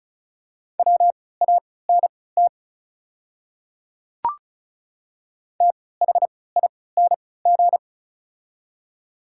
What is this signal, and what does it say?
Morse code, heard as a keyed tone, about 36 wpm, spaced as for 20 wpm, 700 Hz. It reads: WANT THING
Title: 3x Character Spacing - Top 200 Words - 35wpm